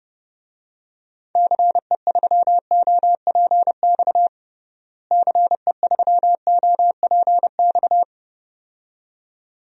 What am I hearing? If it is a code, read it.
Morse code, heard as a keyed tone, about 30 wpm, 700 Hz. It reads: CE3OPX CE3OPX
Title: DXCC-100 Call Signs - 6 Characters - Repeated 2x - 30wpm